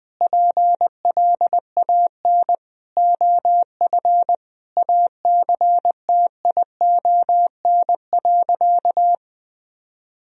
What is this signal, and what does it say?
Morse code, heard as a keyed tone, about 20 wpm, 700 Hz. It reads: PLAN OF ACTION.
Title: Sentences from Top 1000 Words - Review Part 4 - 20wpm